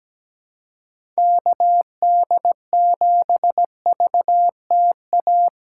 Morse code, heard as a keyed tone, about 17 wpm, 700 Hz. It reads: KD7VTA
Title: USA Call Signs - 4x Character Spacing - 17wpm